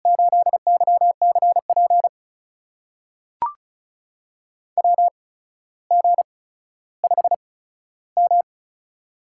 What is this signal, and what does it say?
Morse code, heard as a keyed tone, about 36 wpm, spaced as for 15 wpm, 700 Hz. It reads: 8YCP WG5M
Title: USA Call Signs - 8x Character Spacing - 35wpm